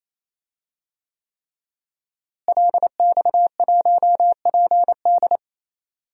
Morse code, heard as a keyed tone, about 28 wpm, 700 Hz. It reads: LX1PD
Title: DXCC-100 Call Signs - 5 Characters - Repeated 3x - 28wpm